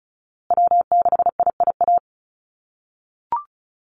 Morse code, heard as a keyed tone, about 34 wpm, 700 Hz. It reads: W6IIA
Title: USA Call Signs Repeated 2x 35wpm